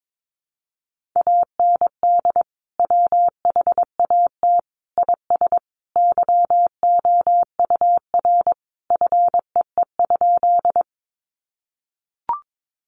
Morse code, heard as a keyed tone, about 22 wpm, 700 Hz. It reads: AND WHAT IS YOUR FEE?
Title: Sentences from Top 1500 Words 22wpm